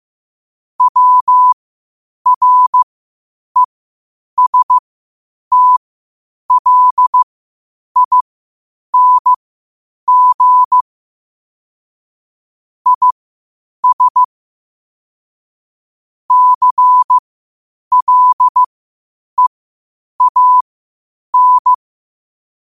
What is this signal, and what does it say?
Morse code, heard as a keyed tone, about 15 wpm, spaced as for 8 wpm, 1000 Hz. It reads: WRESTLING IS CLEAN
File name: qotdfiles / 8wpm